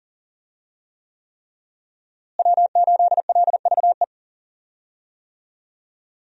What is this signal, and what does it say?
Morse code, heard as a keyed tone, about 37 wpm, 690 Hz. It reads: W8LVE
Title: USA Call Signs - 5 Characters - Repeated 3x - 40wpm